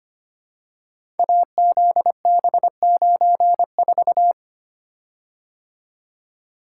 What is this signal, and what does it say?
Morse code, heard as a keyed tone, about 25 wpm, 700 Hz. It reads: AZB94
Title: Code Groups - 5 Letter:Numbers - 25wpm